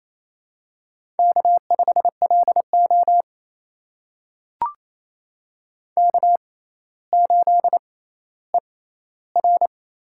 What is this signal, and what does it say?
Morse code, heard as a keyed tone, about 28 wpm, spaced as for 14 wpm, 700 Hz. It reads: K5LO K8ER
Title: USA Call Signs - 4 Characters - 6x Character Spacing - 28wpm